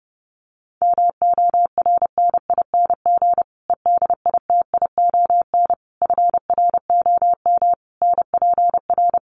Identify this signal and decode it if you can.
Morse code, heard as a keyed tone, about 28 wpm, 700 Hz. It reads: MORNING EDITION FROM NPR
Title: Sets of 5 Words Encore 30wpm